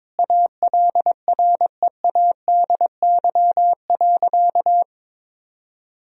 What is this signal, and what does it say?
Morse code, heard as a keyed tone, about 22 wpm, 700 Hz. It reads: ALREADY.
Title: Sentences from Top 1000 Words - Review Part 3 - 22wpm